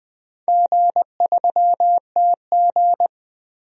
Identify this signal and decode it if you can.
Morse code, heard as a keyed tone, about 21 wpm, 700 Hz. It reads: G3TG
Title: DXCC-100 Call Signs - 4 Characters - Repeated 2x - 20wpm